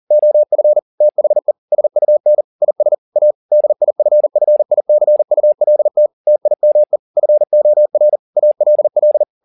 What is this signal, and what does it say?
Morse code, heard as a keyed tone, about 40 wpm, 600 Hz. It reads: OF THE SUN IS A DIFFICULT TIME FOR ALL
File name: the old man and the sea 2 40wpm